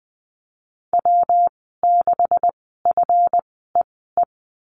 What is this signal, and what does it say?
Morse code, heard as a keyed tone, about 20 wpm, spaced as for 14 wpm, 700 Hz. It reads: W6FEE
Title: 193 Lesson 31g - Letter v - Call Signs 20wpm